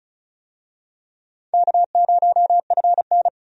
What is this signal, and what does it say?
Morse code, heard as a keyed tone, about 36 wpm, 700 Hz. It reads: K0FN
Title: USA Call Signs - 4 Characters - 35wpm